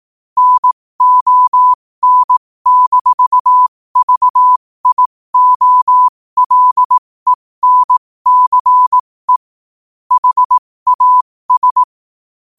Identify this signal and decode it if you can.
Morse code, heard as a keyed tone, about 18 wpm, spaced as for 15 wpm, 1000 Hz. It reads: NON-VIOLENCE HAS
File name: qotdfiles / 15wpm